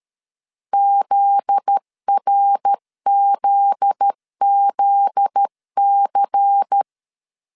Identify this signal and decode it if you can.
Morse code, heard as a keyed tone, about 13 wpm, 800 Hz. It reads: ZRZZC
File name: LEZIONI / 7 GF9Z / 7 GF9Z 12wpm 2147